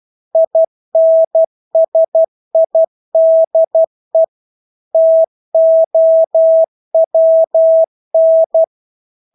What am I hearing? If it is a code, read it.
Morse code, heard as a keyed tone, about 12 wpm, 650 Hz. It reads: INSIDE TOWN
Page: Engelske ord 12 wpm